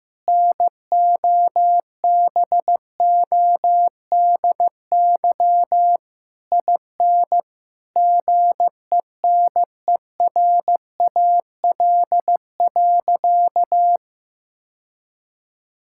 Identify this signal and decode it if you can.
Morse code, heard as a keyed tone, about 16 wpm, 700 Hz. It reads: NOBODY IN GENERAL.
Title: Sentences from Top 1600 Words 15wpm